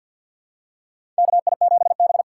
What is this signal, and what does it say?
Morse code, heard as a keyed tone, about 47 wpm, 700 Hz. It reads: KI7B
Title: USA Call Signs - 4 Characters - 6x Character Spacing - 50wpm